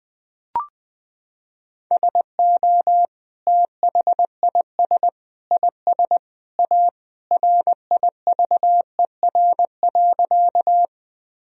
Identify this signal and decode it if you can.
Morse code, heard as a keyed tone, about 20 wpm, 700 Hz. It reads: SO THIS IS A RIVER.
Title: Sentences from Top 1000 Words - Review Part 1 - 20wpm